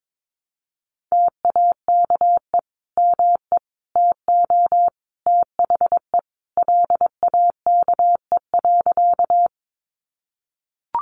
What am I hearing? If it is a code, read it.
Morse code, heard as a keyed tone, about 22 wpm, 700 Hz. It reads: TAKE ME TO THE LAKE.
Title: Sentences from Top 1100 Words 22wpm